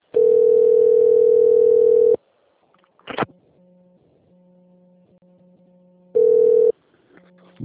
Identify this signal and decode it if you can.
Ringback tone (North America), 440 + 480 Hz.